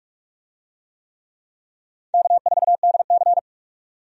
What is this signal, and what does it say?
Morse code, heard as a keyed tone, about 44 wpm, 700 Hz. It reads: K4DC
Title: USA Call Signs - 4 Characters - 45wpm